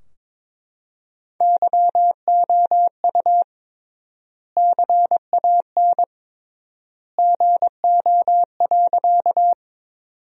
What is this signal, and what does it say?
Morse code, heard as a keyed tone, about 22 wpm, 700 Hz. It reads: YOU CAN GO.
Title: 3x Word-Spacing - Sentences from Top 100 Words - 22wpm